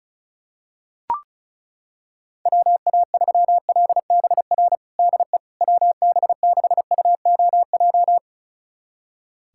Morse code, heard as a keyed tone, about 34 wpm, 700 Hz. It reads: WA3LBR DE WB6UOJ
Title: USA Callsign DE Callsign - 6 Characters - 35wpm